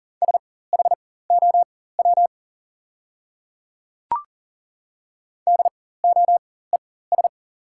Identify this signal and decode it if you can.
Morse code, heard as a keyed tone, about 40 wpm, spaced as for 19 wpm, 700 Hz. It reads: SHOW DOES
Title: 4x Character Spacing - Top 200 Words - 40wpm